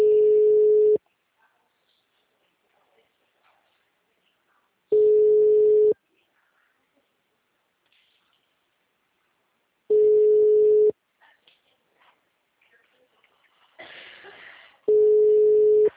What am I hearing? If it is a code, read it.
Ringback tone (Europe, 425 Hz), with speech.
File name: out- -2006- - - .456